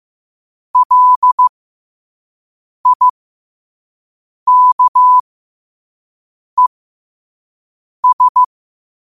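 Morse code, heard as a keyed tone, about 15 wpm, spaced as for 5 wpm, 1000 Hz. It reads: LIKES
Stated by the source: Morse code sent at 5 wpm for learning.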